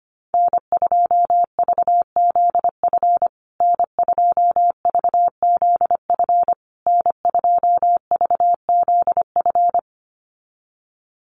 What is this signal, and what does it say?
Morse code, heard as a keyed tone, about 25 wpm, 700 Hz. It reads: N2VZF N2VZF N2VZF
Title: USA Call Signs Repeated 3x 25wpm